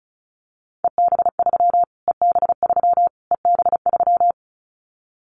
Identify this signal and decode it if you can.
Morse code, heard as a keyed tone, about 34 wpm, 700 Hz. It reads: EB3 EB3 EB3
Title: Callsign Prefixes for Top 25 DXCC - Repeated 3x - 35wpm